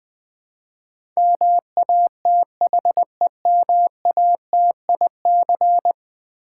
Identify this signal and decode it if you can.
Morse code, heard as a keyed tone, about 19 wpm, 700 Hz. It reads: MATHEMATIC
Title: Build A Word - 5 Letters At A Time - 12-Letter Words - 20wpm